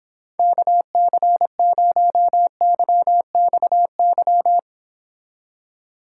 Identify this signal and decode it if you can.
Morse code, heard as a keyed tone, about 26 wpm, 700 Hz. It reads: KC0YXY